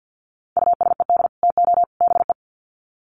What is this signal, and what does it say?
Morse code, heard as a keyed tone, about 50 wpm, 700 Hz. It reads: USED TO BE
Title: Sets of 3 Words 50wpm